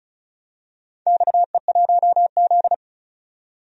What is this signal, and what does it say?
Morse code, heard as a keyed tone, about 36 wpm, 700 Hz. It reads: XE1Z
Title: DXCC-100 Call Signs - 4 Characters - 35wpm